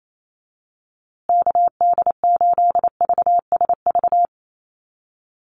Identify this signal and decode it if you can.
Morse code, heard as a keyed tone, about 28 wpm, 700 Hz. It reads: KD8VSV
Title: USA Call Signs Repeated 3x 28wpm